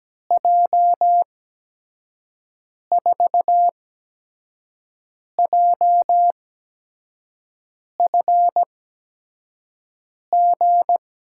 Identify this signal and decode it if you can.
Morse code, heard as a keyed tone, about 17 wpm, spaced as for 5 wpm, 700 Hz. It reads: J4JFG
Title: USA Call Signs - 6 Characters - 8x Character Spacing - 17wpm